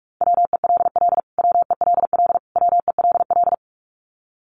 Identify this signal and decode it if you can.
Morse code, heard as a keyed tone, about 44 wpm, 700 Hz. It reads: WELL WELL WELL
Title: Top 200 Words Repeated 45wpm